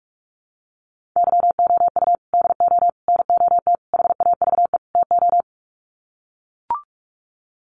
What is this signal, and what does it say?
Morse code, heard as a keyed tone, about 44 wpm, 700 Hz. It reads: YOU DO NOT HAVE TO
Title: Sets of 5 Words 45wpm